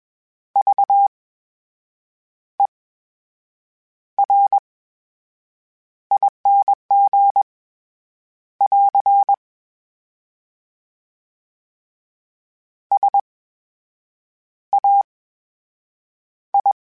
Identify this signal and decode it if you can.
Morse code, heard as a keyed tone, about 21 wpm, spaced as for 7 wpm, 800 Hz. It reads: VERING <AR> SAI
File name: Special INGs-mp3s / Special INGs-mp3s / VerySlowToFaster Ing @20